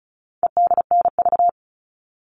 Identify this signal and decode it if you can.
Morse code, heard as a keyed tone, about 34 wpm, 700 Hz. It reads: EDNV
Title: All Cut Numbers - 4-Digit 35wpm